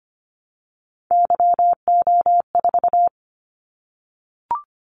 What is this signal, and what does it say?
Morse code, heard as a keyed tone, about 25 wpm, 700 Hz. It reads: YO4
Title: Callsign Prefixes for Top 25 DXCC - 25wpm